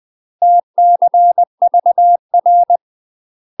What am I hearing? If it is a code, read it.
Morse code, heard as a keyed tone, about 20 wpm, 700 Hz. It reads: TCVR
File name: 20WPM